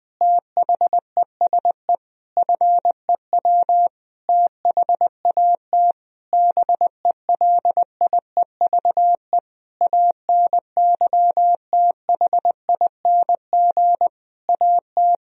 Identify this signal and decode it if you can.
Morse code, heard as a keyed tone, about 20 wpm, 700 Hz. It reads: THESE FEW THAT BELIEVE ANYTHING AT
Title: Sentences from Top 1000 Words - Review Part 1 - 20wpm